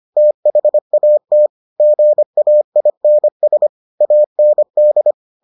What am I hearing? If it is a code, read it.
Morse code, heard as a keyed tone, about 25 wpm, 600 Hz. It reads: THAT GAINS AND